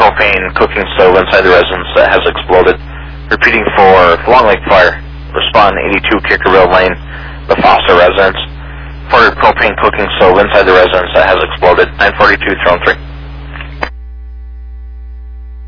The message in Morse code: none

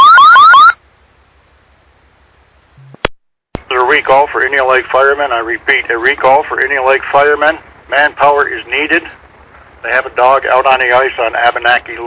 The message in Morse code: none

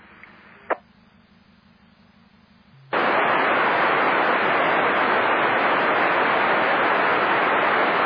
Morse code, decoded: none

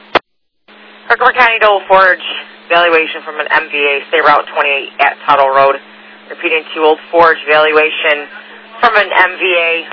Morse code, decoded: none

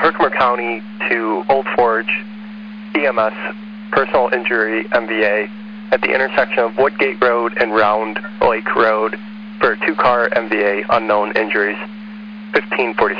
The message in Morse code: none

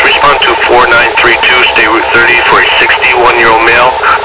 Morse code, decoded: none

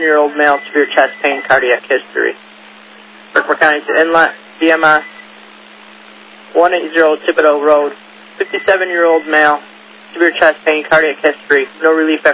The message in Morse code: none

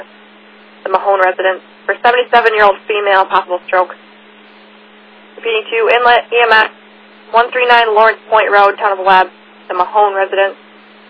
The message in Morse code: none

